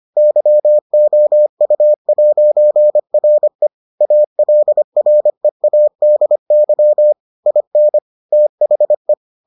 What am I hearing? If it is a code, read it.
Morse code, heard as a keyed tone, about 25 wpm, 600 Hz. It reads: YOU'RE ALREADY IN THE